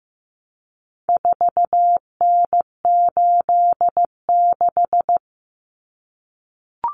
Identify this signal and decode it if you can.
Morse code, heard as a keyed tone, about 15 wpm, 700 Hz. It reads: 4N86
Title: Common Cut Numbers - 4-Digit 15wpm